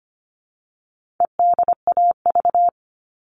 Morse code, heard as a keyed tone, about 25 wpm, 700 Hz. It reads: EDAV
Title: All Cut Numbers - 4-Digit 25wpm